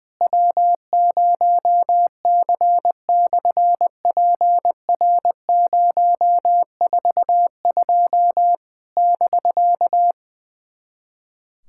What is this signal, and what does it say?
Morse code, heard as a keyed tone, about 20 wpm, 700 Hz. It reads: W0C/PR042 <BK>
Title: SOTA QSO Practice 20wpm